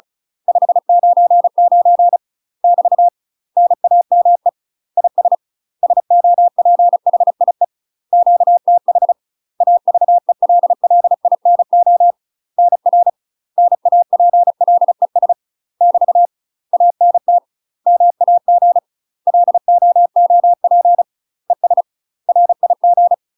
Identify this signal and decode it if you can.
Morse code, heard as a keyed tone, about 36 wpm, 700 Hz. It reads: 599 <BT> NAME IS SOPHIE QTH AVELLINO NR NAPLES <BT> ANT MAG LOOP ES RIG
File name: UK MASTER QSOs TRAINING / 35WPM / UK MASTER QSOs 0009